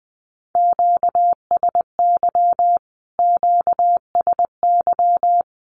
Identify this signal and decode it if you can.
Morse code, heard as a keyed tone, about 20 wpm, 700 Hz. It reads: QSY QSY
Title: Q-Codes Repeated 3x 20wpm